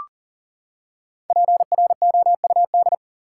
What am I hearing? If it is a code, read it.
Morse code, heard as a keyed tone, about 40 wpm, 700 Hz. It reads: PROUD
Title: Five Letter Words - 40wpm